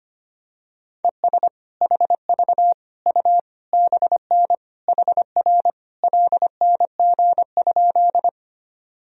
Morse code, heard as a keyed tone, about 25 wpm, 700 Hz. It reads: ES HV U BN HR LNG?
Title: Sentences from Top 200 Words and Abbreviations 25wpm